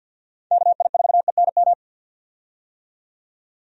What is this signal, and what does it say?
Morse code, heard as a keyed tone, about 52 wpm, 700 Hz. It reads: KI4ENM
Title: USA Call Signs - 6 Characters - 50wpm